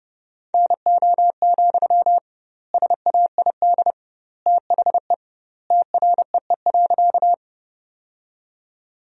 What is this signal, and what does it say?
Morse code, heard as a keyed tone, about 32 wpm, 700 Hz. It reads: NO, SAID THE TREE.
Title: 2x Word-Spacing - Sentences from Top 300 Words - 30wpm